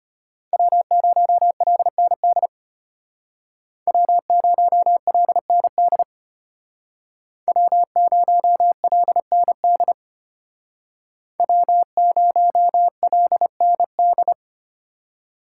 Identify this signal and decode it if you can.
Morse code, heard as a keyed tone, about 33 wpm, 700 Hz. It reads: W0LND W0LND W0LND W0LND